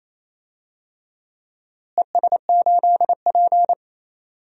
Morse code, heard as a keyed tone, about 27 wpm, 700 Hz. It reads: ES8P
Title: DXCC-100 Call Signs - 4 Characters - Repeated 2x - 28wpm